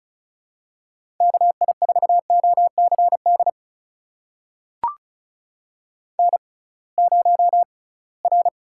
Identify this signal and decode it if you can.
Morse code, heard as a keyed tone, about 36 wpm, spaced as for 23 wpm, 700 Hz. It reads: KI4OCD N0R
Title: USA Call Signs - 6x Character Spacing - 35wpm